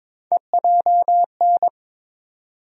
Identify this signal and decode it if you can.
Morse code, heard as a keyed tone, about 22 wpm, 700 Hz. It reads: EJN
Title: USA Call Sign Postfixes - 3 Letters 22wpm